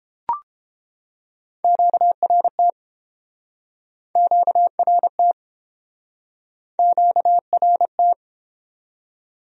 Morse code, heard as a keyed tone, about 29 wpm, 700 Hz. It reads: QRT QRT QRT